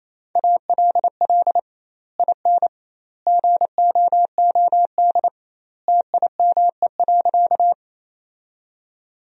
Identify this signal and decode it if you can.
Morse code, heard as a keyed tone, about 28 wpm, 700 Hz. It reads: ALL IN GOOD TIME.
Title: 2x Word-Spacing - Sentences from Top 200 Words - 28wpm